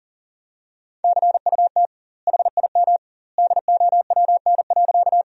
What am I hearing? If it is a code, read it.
Morse code, heard as a keyed tone, about 40 wpm, 700 Hz. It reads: CUT HIM DOWN.
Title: 2x Word-Spacing - Sentences from Top 300 Words - 40wpm